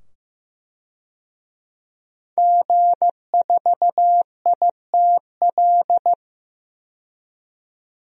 Morse code, heard as a keyed tone, about 15 wpm, 700 Hz. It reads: G4ITL
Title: DXCC-100 Call Signs - 5 Characters - 15wpm